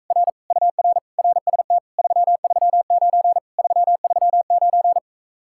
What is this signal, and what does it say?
Morse code, heard as a keyed tone, about 43 wpm, 700 Hz. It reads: R UR RST 339 339